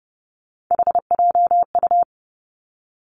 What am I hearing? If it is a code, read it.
Morse code, heard as a keyed tone, about 30 wpm, 700 Hz. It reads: HJU
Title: Three Challenging Letter-Number Combinations 30wpm